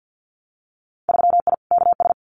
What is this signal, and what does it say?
Morse code, heard as a keyed tone, about 50 wpm, 700 Hz. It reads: 3I KS
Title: Winter Field Day Exchanges Encore 50wpm